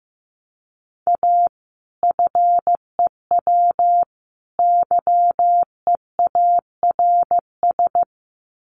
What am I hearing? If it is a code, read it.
Morse code, heard as a keyed tone, about 15 wpm, 700 Hz. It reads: A FEW YEARS
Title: Sets of 3 Words 15wpm